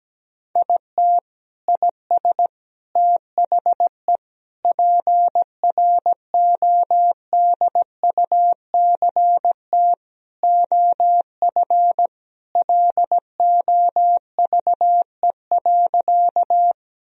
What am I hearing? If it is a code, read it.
Morse code, heard as a keyed tone, about 17 wpm, 700 Hz. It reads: IT IS THE PRODUCT OF LOVE.
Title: Sentences from Top 1000 Words - Review Part 2 - 17wpm